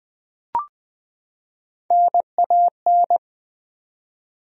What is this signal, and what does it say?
Morse code, heard as a keyed tone, about 20 wpm, 700 Hz. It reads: NAN
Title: USA Common Names - Max 3 Letters - 20wpm